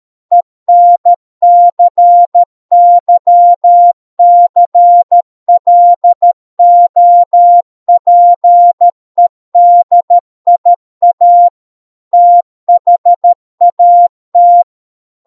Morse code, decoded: ENCYCLOPEDIA THAT